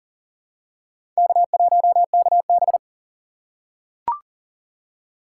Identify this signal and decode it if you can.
Morse code, heard as a keyed tone, about 40 wpm, 700 Hz. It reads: K1KB
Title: USA Call Signs - 4 Characters - 3x Character Spacing - 40wpm